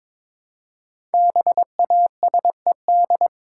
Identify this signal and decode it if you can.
Morse code, heard as a keyed tone, about 22 wpm, 700 Hz. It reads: BASED
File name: 5 Letter Words - First 3 Letters Given - 22wpm